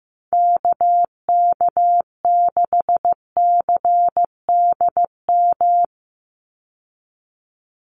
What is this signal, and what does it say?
Morse code, heard as a keyed tone, about 15 wpm, 700 Hz. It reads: KK6CDM
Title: USA Call Signs 15wpm